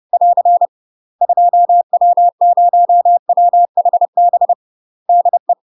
Morse code, heard as a keyed tone, about 30 wpm, 700 Hz. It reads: <AR> 2W0WHB DE
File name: UK MASTER QSOs TRAINING / 30WPM / UK MASTER QSOs 0007